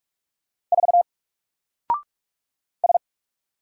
Morse code, heard as a keyed tone, about 44 wpm, 700 Hz. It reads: 4 S
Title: Single Character (Letter:Number:Punctuation:Prosign) 45wpm